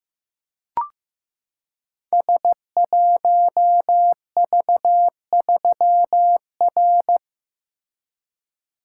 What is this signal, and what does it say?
Morse code, heard as a keyed tone, about 15 wpm, 700 Hz. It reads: S1V3R
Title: Code Groups - 5 Letter:Numbers - 15wpm